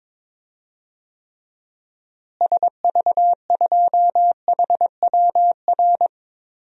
Morse code, heard as a keyed tone, about 21 wpm, 700 Hz. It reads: SV2HWR
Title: DXCC-100 Call Signs - 6 Characters - Repeated 2x - 22wpm